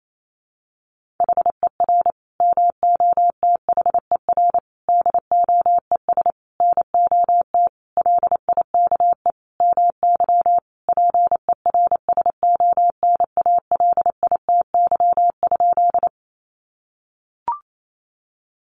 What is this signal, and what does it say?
Morse code, heard as a keyed tone, about 28 wpm, 700 Hz. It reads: HER MOTHER DOES NOT LIKE MY PERSONALITY?
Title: Sentences from Top 1400 Words 28wpm